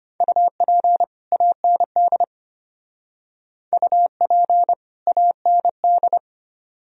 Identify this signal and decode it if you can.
Morse code, heard as a keyed tone, about 31 wpm, 700 Hz. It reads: UP AND UP AND
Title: Speed-Racing - Sets of 2 Words 25wpm